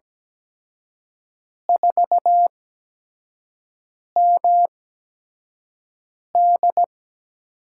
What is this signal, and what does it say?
Morse code, heard as a keyed tone, about 17 wpm, spaced as for 5 wpm, 700 Hz. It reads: 4MD
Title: USA Call Signs - 4 Characters - 8x Character Spacing - 17wpm